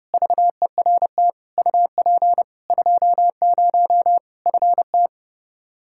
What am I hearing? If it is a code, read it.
Morse code, heard as a keyed tone, about 32 wpm, 700 Hz. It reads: VERT UP 20 FT
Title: Practice QSO - Exchange 2 - Info - 30wpm